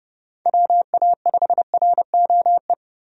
Morse code, heard as a keyed tone, about 32 wpm, 700 Hz. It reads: WA5ROE
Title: USA Call Signs - 6 Characters - Repeated 3x - 30wpm